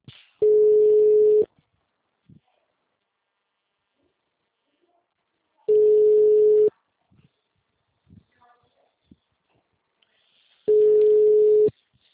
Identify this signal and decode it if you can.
Ringback tone (Europe, 425 Hz).